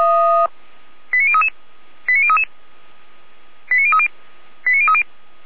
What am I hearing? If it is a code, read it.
Five-tone selective call (ZVEI), heard as quick runs of five tones, tones of about 70 ms. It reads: 89033 89033 89030 89030